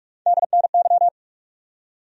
Morse code, heard as a keyed tone, about 46 wpm, 700 Hz. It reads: DNY
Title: USA Call Sign Postfixes - 3 Letters 45wpm